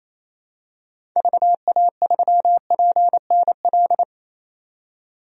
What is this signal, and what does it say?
Morse code, heard as a keyed tone, about 28 wpm, 700 Hz. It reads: VA3PNL
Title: DXCC-100 Call Signs - 6 Characters - 28wpm